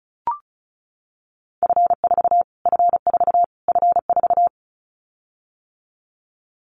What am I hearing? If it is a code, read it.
Morse code, heard as a keyed tone, about 34 wpm, 700 Hz. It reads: F4 F4 F4
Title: Callsign Prefixes for Top 25 DXCC - Repeated 3x - 35wpm